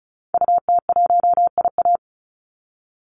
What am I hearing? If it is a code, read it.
Morse code, heard as a keyed tone, about 34 wpm, 700 Hz. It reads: UT1IA
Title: Call Signs Repeated 3x - 35wpm